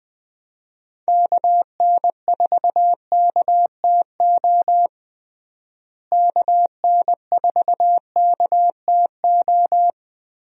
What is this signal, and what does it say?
Morse code, heard as a keyed tone, about 19 wpm, 700 Hz. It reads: KN4KTO KN4KTO
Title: USA Call Signs - 6 Characters - Repeated 3x - 20wpm